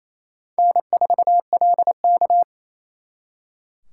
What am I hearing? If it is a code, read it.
Morse code, heard as a keyed tone, about 34 wpm, 700 Hz. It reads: N4LK
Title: Speed Racing - USA 1x2 Call Signs 28wpm